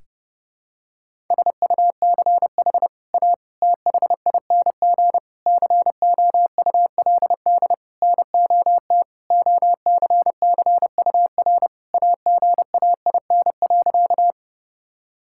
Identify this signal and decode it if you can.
Morse code, heard as a keyed tone, about 30 wpm, 700 Hz. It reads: SUCH A THING COULD NOT OCCUR AGAIN.
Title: Sentences from Top 1000 Words - Review Part 6 - 30wpm